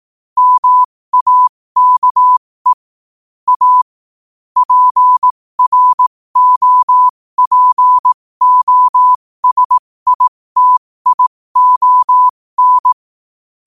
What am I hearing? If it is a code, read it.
Morse code, heard as a keyed tone, about 18 wpm, spaced as for 15 wpm, 1000 Hz. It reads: MAKE A PROPOSITION
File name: qotdfiles / 15wpm